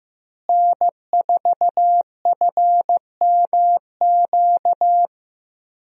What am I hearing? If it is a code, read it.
Morse code, heard as a keyed tone, about 15 wpm, 700 Hz. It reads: N4FMQ
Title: USA Call Signs - 5 Characters - Repeated 3x - 15wpm